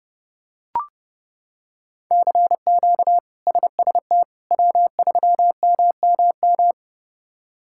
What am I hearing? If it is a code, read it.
Morse code, heard as a keyed tone, about 29 wpm, 700 Hz. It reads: CQ SST W3MMM
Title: K1USN SST QSO Practice - 30wpm